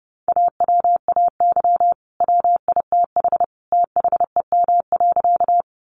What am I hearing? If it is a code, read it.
Morse code, heard as a keyed tone, about 30 wpm, 700 Hz. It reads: AWAY WITH THEM.
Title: Sentences from Top 600 Words 30wpm